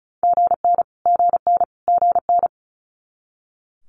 Morse code, heard as a keyed tone, about 36 wpm, 700 Hz. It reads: GN GN GN
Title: Multi-letter Phonemes Repeated 3x 35wpm